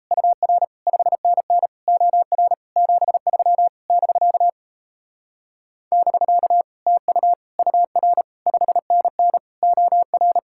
Morse code, heard as a keyed tone, about 38 wpm, 700 Hz. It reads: UR 5NN OR 73 <BK> <BK> TU UR 5NN OR